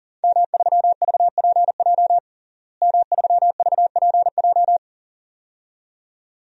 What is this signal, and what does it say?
Morse code, heard as a keyed tone, about 40 wpm, 700 Hz. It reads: M3VPJ M3VPJ
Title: DXCC-100 Call Signs - 5 Characters - Repeated 2x - 40wpm